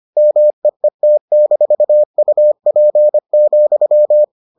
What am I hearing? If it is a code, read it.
Morse code, heard as a keyed tone, about 25 wpm, 600 Hz. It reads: MEET-UP,